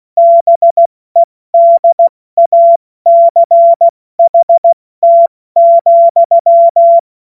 Morse code, heard as a keyed tone, about 16 wpm, 680 Hz. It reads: BEDACHT,
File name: Morsekurs / Witze / Witz 10. Mai 2022 70 F80